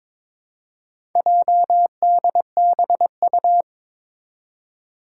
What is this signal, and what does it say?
Morse code, heard as a keyed tone, about 22 wpm, 700 Hz. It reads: JDBU
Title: Four Challenging Letter-Number Combinations 22wpm